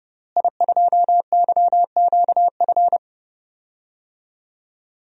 Morse code, heard as a keyed tone, about 31 wpm, 700 Hz. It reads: I2YQF
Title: DXCC-100 Call Signs - 5 Characters - Repeated 3x - 30wpm